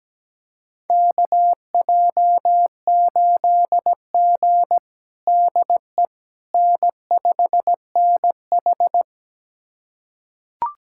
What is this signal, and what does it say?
Morse code, heard as a keyed tone, about 17 wpm, 700 Hz. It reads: KJ8G DE N5NH
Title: USA Callsign DE Callsign - 4 Characters - 17wpm